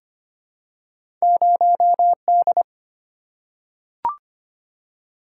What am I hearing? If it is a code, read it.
Morse code, heard as a keyed tone, about 25 wpm, 700 Hz. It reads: 0D
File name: Two Letter-Number Combinations - 25wpm